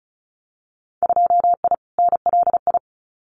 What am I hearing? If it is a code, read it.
Morse code, heard as a keyed tone, about 34 wpm, 700 Hz. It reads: 2I NLI
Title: Winter Field Day Exchanges 35wpm